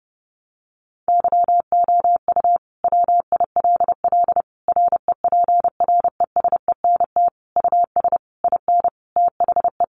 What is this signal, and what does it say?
Morse code, heard as a keyed tone, about 30 wpm, 700 Hz. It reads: YOU WILL REPRESENT US IN THE
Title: Sentences from Top 1200 Words 30wpm